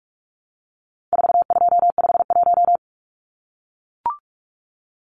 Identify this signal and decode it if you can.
Morse code, heard as a keyed tone, about 44 wpm, 700 Hz. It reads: 4251
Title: Numbers - 4-Digit 45wpm